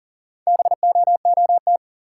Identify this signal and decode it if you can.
Morse code, heard as a keyed tone, about 40 wpm, 700 Hz. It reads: BOOT
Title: Four Letter Words - 40wpm